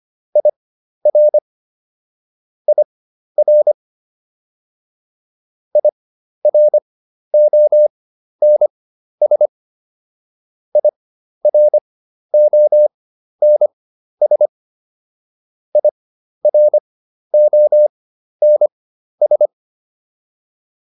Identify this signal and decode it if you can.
Morse code, heard as a keyed tone, about 25 wpm, spaced as for 12 wpm, 600 Hz. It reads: IR IR IRONS IRONS IRONS